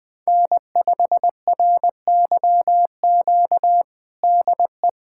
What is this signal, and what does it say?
Morse code, heard as a keyed tone, about 20 wpm, 700 Hz. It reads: N5RYQ DE
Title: USA Callsign DE Callsign - 5 Characters - 20wpm